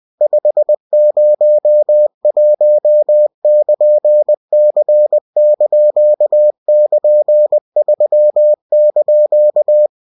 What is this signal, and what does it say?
Morse code, heard as a keyed tone, about 20 wpm, 600 Hz. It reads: 501<KN>C)<KN>3)